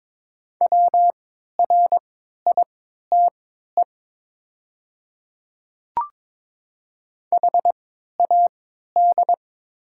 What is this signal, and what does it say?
Morse code, heard as a keyed tone, about 22 wpm, spaced as for 12 wpm, 700 Hz. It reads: WRITE HAD
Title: 3x Character Spacing - Top 200 Words - 22wpm